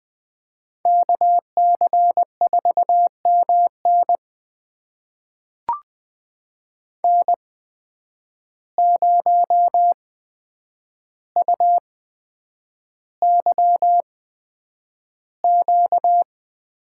Morse code, heard as a keyed tone, about 20 wpm, spaced as for 9 wpm, 700 Hz. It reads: KC4MN N0UYQ
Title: USA Call Signs - 5 Characters - 8x Character Spacing - 20wpm